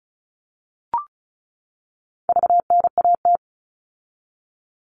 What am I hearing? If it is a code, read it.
Morse code, heard as a keyed tone, about 34 wpm, 700 Hz. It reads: VNAT
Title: All Cut Numbers - 4-Digit 35wpm